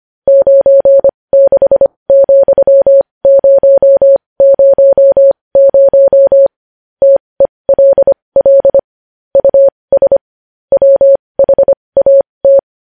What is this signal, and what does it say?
Morse code, heard as a keyed tone, about 25 wpm, spaced as for 20 wpm, 550 Hz. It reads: 96,000 TELL US WHAT
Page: news-headlines-in-morse-code-at-20-wpm